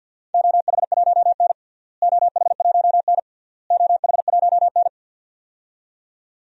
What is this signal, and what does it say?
Morse code, heard as a keyed tone, about 46 wpm, 700 Hz. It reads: OH1N OH1N OH1N
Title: DXCC-100 Call Signs - 4 Characters - Repeated 3x - 50wpm